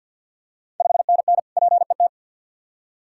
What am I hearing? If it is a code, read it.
Morse code, heard as a keyed tone, about 50 wpm, 700 Hz. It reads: 5NN PET
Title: Ontario QSO Party Exchange 50wpm